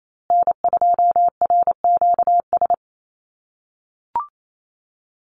Morse code, heard as a keyed tone, about 28 wpm, 700 Hz. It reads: N2RQS
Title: USA Call Signs Repeated 2x 28wpm